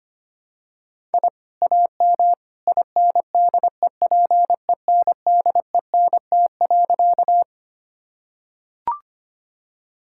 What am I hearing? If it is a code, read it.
Morse code, heard as a keyed tone, about 25 wpm, 700 Hz. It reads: I AM INDEPENDENT.
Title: Sentences from Top 1000 Words - Review Part 6 - 25wpm